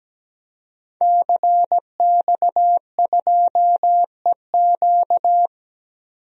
Morse code, heard as a keyed tone, about 17 wpm, 700 Hz. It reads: CX2EQ
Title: DXCC-100 Call Signs - 5 Characters - 17wpm